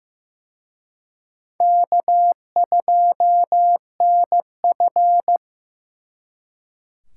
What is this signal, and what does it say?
Morse code, heard as a keyed tone, about 18 wpm, 700 Hz. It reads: K2NF
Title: Speed Racing - USA 1x2 Call Signs 15wpm